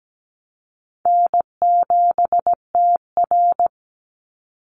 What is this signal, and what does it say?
Morse code, heard as a keyed tone, about 17 wpm, 700 Hz. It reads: N7TR
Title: Easy Call Signs - 17wpm